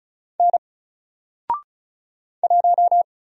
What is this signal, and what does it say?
Morse code, heard as a keyed tone, about 35 wpm, 700 Hz. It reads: N 1
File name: Single-Letter-Number-35wpm